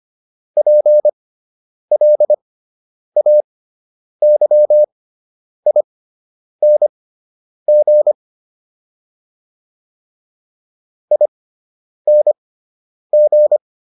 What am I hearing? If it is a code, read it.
Morse code, heard as a keyed tone, about 25 wpm, spaced as for 9 wpm, 600 Hz. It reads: PLAYING ING